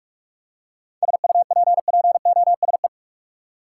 Morse code, heard as a keyed tone, about 42 wpm, 700 Hz. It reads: SUPPOSE
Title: Seven Letter Words - 45wpm